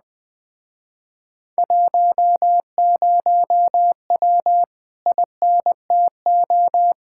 Morse code, heard as a keyed tone, about 20 wpm, 700 Hz. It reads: 10W INTO
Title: Speed-Racing Common QSO Phrases - 17wpm